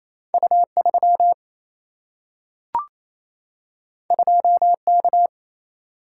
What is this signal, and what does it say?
Morse code, heard as a keyed tone, about 28 wpm, 700 Hz. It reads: U3 2K
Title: Single-Letter Single-Number 28wpm